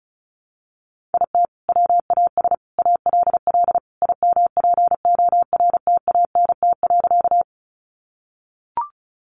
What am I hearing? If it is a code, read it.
Morse code, heard as a keyed tone, about 34 wpm, 700 Hz. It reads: IT WAS ALL IMPORTANT.
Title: Sentences from Top 600 Words 35wpm